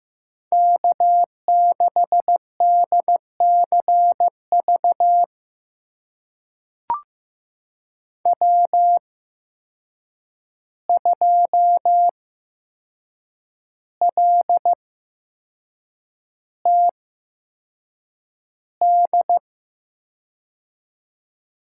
Morse code, heard as a keyed tone, about 15 wpm, spaced as for 6 wpm, 700 Hz. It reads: K6DCV W2LTD
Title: USA Call Signs - 5 Characters - 8x Character Spacing - 15wpm